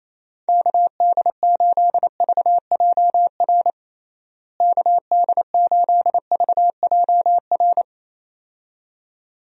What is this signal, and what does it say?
Morse code, heard as a keyed tone, about 28 wpm, 700 Hz. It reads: KD8VJR KD8VJR
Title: USA Call Signs - 6 Characters - Repeated 2x - 28wpm